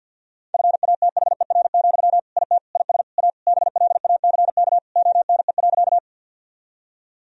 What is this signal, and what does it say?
Morse code, heard as a keyed tone, about 52 wpm, 700 Hz. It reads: FATHER, IT IS A BLACK ONE.
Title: Sentences from Top 1000 Words - Review Part 2 - 50wpm